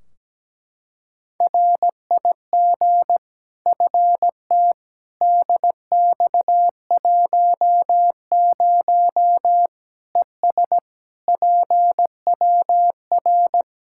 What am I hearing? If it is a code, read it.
Morse code, heard as a keyed tone, about 17 wpm, 700 Hz. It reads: RIG FT DX10 ES PWR
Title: Practice QSO - Exchange 2 - Info - 17wpm